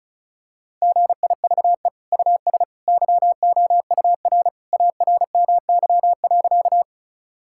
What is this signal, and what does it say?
Morse code, heard as a keyed tone, about 36 wpm, 700 Hz. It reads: GIVE US YOUR ARMY.
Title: Sentences from Top 1000 Words - Review Part 5 - 35wpm